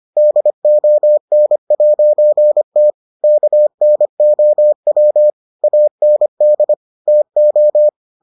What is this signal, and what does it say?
Morse code, heard as a keyed tone, about 25 wpm, 600 Hz. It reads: DON'T KNOW AND TO